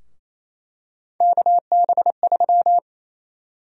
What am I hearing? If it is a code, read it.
Morse code, heard as a keyed tone, about 28 wpm, 700 Hz. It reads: KB3
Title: USA Call Sign Prefixes - Repeated 2x - 28wpm